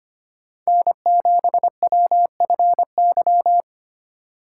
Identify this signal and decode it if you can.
Morse code, heard as a keyed tone, about 30 wpm, 700 Hz. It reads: N7WFY
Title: Speed Racing - USA 1x3 Call Signs 25wpm